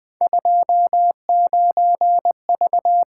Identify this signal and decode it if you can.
Morse code, heard as a keyed tone, about 20 wpm, 700 Hz. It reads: 29V
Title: Three Letter-Number Combinations - 20wpm